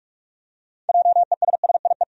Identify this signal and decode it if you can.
Morse code, heard as a keyed tone, about 46 wpm, 700 Hz. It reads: JESSIE
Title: USA Common Names - 45wpm